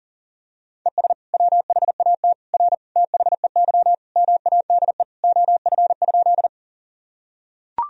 Morse code, heard as a keyed tone, about 40 wpm, 700 Hz. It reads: ES WHAT R THEY MADE OF?
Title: Sentences from Top 200 Words and Abbreviations 40wpm